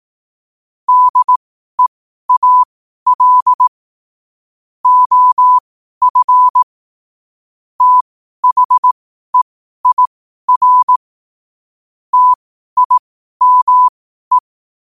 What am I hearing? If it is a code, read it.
Morse code, heard as a keyed tone, about 18 wpm, spaced as for 12 wpm, 1000 Hz. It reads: DEAL OF THEIR TIME
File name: qotdfiles / 12wpm